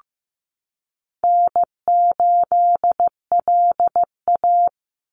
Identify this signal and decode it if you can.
Morse code, heard as a keyed tone, about 15 wpm, 700 Hz. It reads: N8LA
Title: Easy Call Signs Repeated 3x - 15wpm